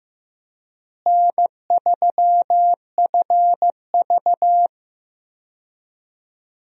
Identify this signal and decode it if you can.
Morse code, heard as a keyed tone, about 15 wpm, 700 Hz. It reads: N3FV
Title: USA Call Signs - 4 Characters - Repeated 3x - 15wpm